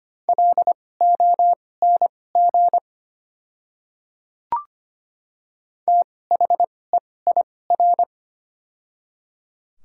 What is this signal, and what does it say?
Morse code, heard as a keyed tone, about 25 wpm, spaced as for 18 wpm, 700 Hz. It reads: LONG THEIR
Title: 2x Character Spacing - Top 100 Words - 25wpm